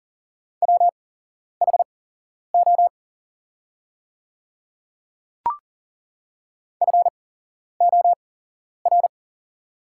Morse code, heard as a keyed tone, about 39 wpm, spaced as for 11 wpm, 700 Hz. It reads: WHO FOR
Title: 8x Character Spacing - Top 100 Words - 40wpm